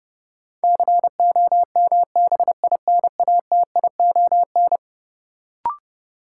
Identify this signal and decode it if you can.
Morse code, heard as a keyed tone, about 30 wpm, 700 Hz. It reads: COMBINATION
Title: Sentences from Top 1000 Words - Review Part 6 - New Word Drill - 30wpm